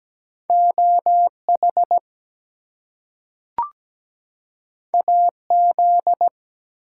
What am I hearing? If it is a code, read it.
Morse code, heard as a keyed tone, about 17 wpm, 700 Hz. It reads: OH AZ
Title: US States and Canadian Provinces 17wpm